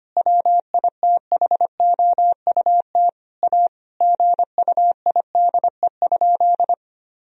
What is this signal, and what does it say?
Morse code, heard as a keyed tone, about 25 wpm, 700 Hz. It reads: WITHOUT A GUIDE?
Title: Sentences from Top 1000 Words - Review Part 3 - 25wpm